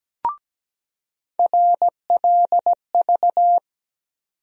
Morse code, heard as a keyed tone, about 17 wpm, 700 Hz. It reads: RLV
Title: USA Call Sign Postfixes - 3 Letters 17wpm